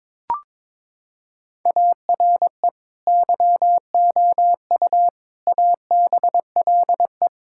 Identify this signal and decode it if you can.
Morse code, heard as a keyed tone, about 22 wpm, 700 Hz. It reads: ARE YOU ABLE
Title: Sentences from Top 1000 Words - Review Part 5 - 22wpm